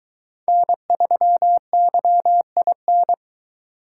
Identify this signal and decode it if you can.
Morse code, heard as a keyed tone, about 23 wpm, 700 Hz. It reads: N3YIN